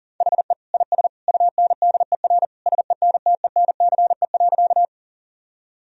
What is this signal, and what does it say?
Morse code, heard as a keyed tone, about 40 wpm, 700 Hz. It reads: HE IS UNDER SENTENCE.
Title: Sentences from Top 1000 Words - Review Part 1 - 40wpm